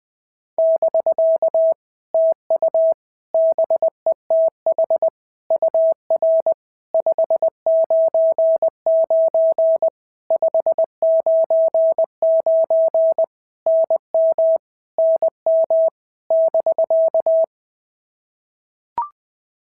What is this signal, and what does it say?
Morse code, heard as a keyed tone, about 19 wpm, 650 Hz. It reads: <BK> TU BETH UR 599 599 NM NM <BK>
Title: POTA QSO Practice - 20wpm